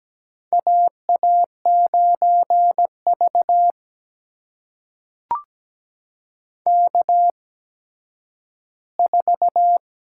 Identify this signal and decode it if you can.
Morse code, heard as a keyed tone, about 17 wpm, spaced as for 10 wpm, 700 Hz. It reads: AA9V K4
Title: USA Call Signs - 4 Characters - 8x Character Spacing - 17wpm